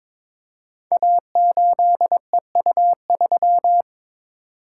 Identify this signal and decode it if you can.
Morse code, heard as a keyed tone, about 22 wpm, 700 Hz. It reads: A8EU3
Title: Code Groups - 5 Letter:Numbers - 22wpm